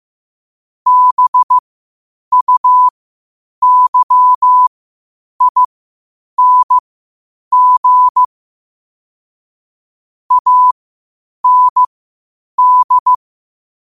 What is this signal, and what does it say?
Morse code, heard as a keyed tone, about 15 wpm, spaced as for 8 wpm, 1000 Hz. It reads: BUYING AND
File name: qotdfiles / 8wpm